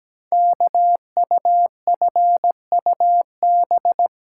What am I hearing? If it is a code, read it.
Morse code, heard as a keyed tone, about 17 wpm, 700 Hz. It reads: KUFUB
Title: Code Groups - 5 Letters - 17wpm